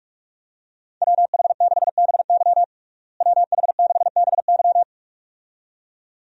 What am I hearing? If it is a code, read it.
Morse code, heard as a keyed tone, about 42 wpm, 700 Hz. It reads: WH6BY WH6BY
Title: USA Call Signs - 5 Characters - Repeated 2x - 45wpm